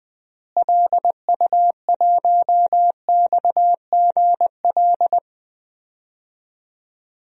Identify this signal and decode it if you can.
Morse code, heard as a keyed tone, about 20 wpm, 700 Hz. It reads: LU1XGL
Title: DXCC-100 Call Signs - 6 Characters - Repeated 3x - 20wpm